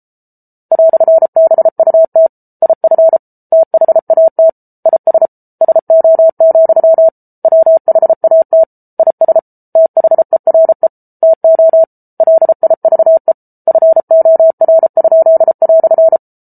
Morse code, heard as a keyed tone, about 34 wpm, 650 Hz. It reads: "BUT IF THAT IS SO, WHAT IS THERE TO LIVE FOR?"